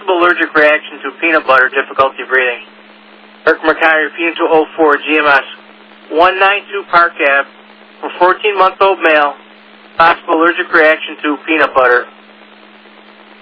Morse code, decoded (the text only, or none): none